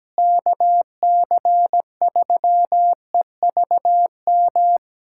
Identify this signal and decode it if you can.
Morse code, heard as a keyed tone, about 17 wpm, 700 Hz. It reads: KC3EVM
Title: USA Callsign DE Callsign 17wpm